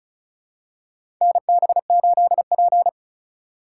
Morse code, heard as a keyed tone, about 36 wpm, 700 Hz. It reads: NB8P
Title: USA Call Signs - 4 Characters - 4x Character Spacing - 35wpm